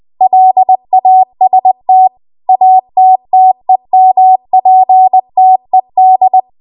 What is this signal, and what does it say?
Morse code, heard as a keyed tone, about 20 wpm, 750 Hz. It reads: LAST ATTEMPTED
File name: Morse / 20WPM